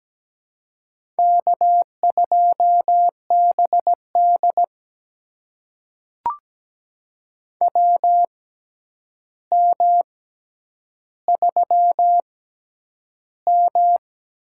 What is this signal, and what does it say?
Morse code, heard as a keyed tone, about 17 wpm, spaced as for 9 wpm, 700 Hz. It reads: K2BD WM3M
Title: USA Call Signs - 4 Characters - 6x Character Spacing - 17wpm